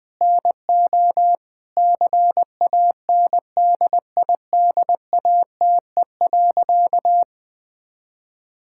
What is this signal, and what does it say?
Morse code, heard as a keyed tone, about 20 wpm, 700 Hz. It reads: NO CANDIDATE.
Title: Sentences from Top 1600 Words 20wpm